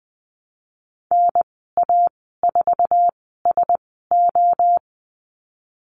Morse code, heard as a keyed tone, about 20 wpm, spaced as for 15 wpm, 700 Hz. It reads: NA4SO
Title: 048 Lesson 9g - Number 4 - Call Signs 20wpm